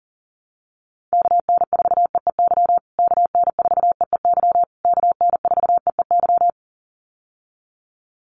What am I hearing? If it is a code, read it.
Morse code, heard as a keyed tone, about 40 wpm, 700 Hz. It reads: KN4EEY KN4EEY KN4EEY
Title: USA Call Signs Repeated 3x 40wpm